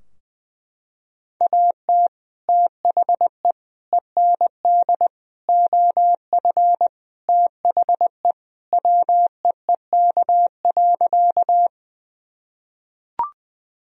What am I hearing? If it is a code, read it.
Morse code, heard as a keyed tone, about 21 wpm, 700 Hz. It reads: AT THE END OF THE WEEK.
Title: Sentences from Top 1000 Words - Review Part 2 - 20wpm